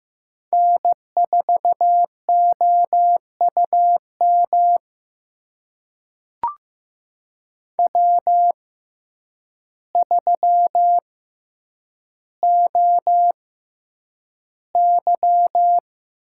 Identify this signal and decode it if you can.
Morse code, heard as a keyed tone, about 15 wpm, spaced as for 8 wpm, 700 Hz. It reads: N4OUM W3OY
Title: USA Call Signs - 5 Characters - 6x Character Spacing - 15wpm